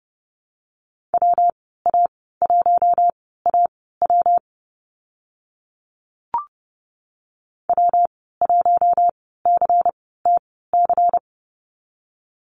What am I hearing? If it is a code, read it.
Morse code, heard as a keyed tone, about 30 wpm, spaced as for 17 wpm, 700 Hz. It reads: WA1AW W1CTC
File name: Lesson 19g - Letter w - Call Signs 30wpm